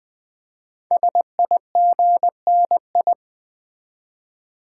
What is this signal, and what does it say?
Morse code, heard as a keyed tone, about 20 wpm, 700 Hz. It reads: SIGNI
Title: Build A Word - 5 Letters At A Time - 12-Letter Words - 20wpm